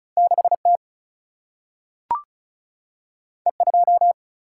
Morse code, heard as a keyed tone, about 36 wpm, 700 Hz. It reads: 6T E2
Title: Single-Letter Single-Number 35wpm